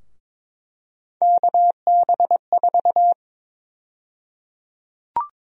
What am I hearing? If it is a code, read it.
Morse code, heard as a keyed tone, about 21 wpm, 700 Hz. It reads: KB4
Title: USA Call Sign Prefixes - 22wpm